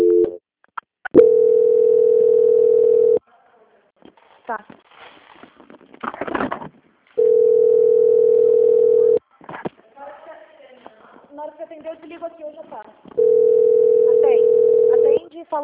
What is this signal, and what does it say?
Ringback tone (North America), 440 + 480 Hz, with speech.
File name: out- -2002- - - .1438